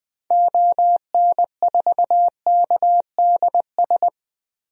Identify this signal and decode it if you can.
Morse code, heard as a keyed tone, about 20 wpm, 700 Hz. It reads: ON4KDS
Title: DXCC-100 Call Signs - 6 Characters - Repeated 3x - 20wpm